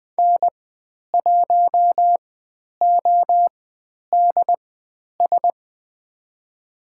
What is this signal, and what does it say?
Morse code, heard as a keyed tone, about 20 wpm, spaced as for 10 wpm, 700 Hz. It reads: N1ODS
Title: 102 Lesson 17g - Letter u - Call Signs 20wpm